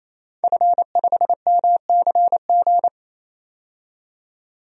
Morse code, heard as a keyed tone, about 28 wpm, 700 Hz. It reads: F5MCG
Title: DXCC-100 Call Signs - 5 Characters - 28wpm